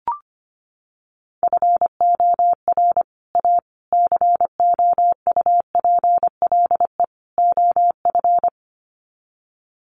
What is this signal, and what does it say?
Morse code, heard as a keyed tone, about 25 wpm, 700 Hz. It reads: FOR A COUPLE OF
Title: Sets of 4 Words 25wpm